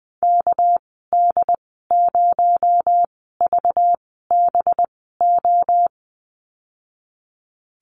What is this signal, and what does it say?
Morse code, heard as a keyed tone, about 20 wpm, spaced as for 15 wpm, 700 Hz. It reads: KD0VBO
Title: 219 Lesson 35g - Number 0 - Call Signs 20wpm